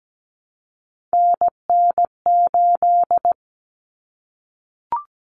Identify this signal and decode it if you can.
Morse code, heard as a keyed tone, about 17 wpm, 700 Hz. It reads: NN8
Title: Common Cut Numbers - 3-Digit 17wpm